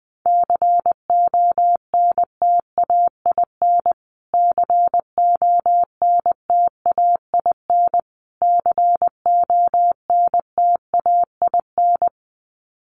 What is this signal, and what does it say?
Morse code, heard as a keyed tone, about 20 wpm, 700 Hz. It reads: CONTAIN CONTAIN CONTAIN
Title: Top 500 Words Repeated 20wpm